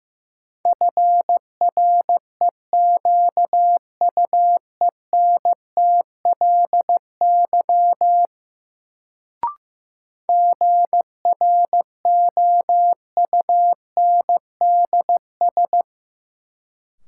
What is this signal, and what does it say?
Morse code, heard as a keyed tone, about 15 wpm, 700 Hz. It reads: FREQUENTLY GROUNDS
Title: Sentences from Top 1300 Words - New Word Drill - 15wpm